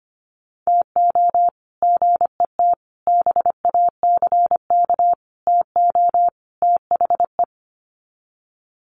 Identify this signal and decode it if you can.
Morse code, heard as a keyed tone, about 25 wpm, 700 Hz. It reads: TO GET BACK TO THE
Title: Sets of 5 Words Encore 25wpm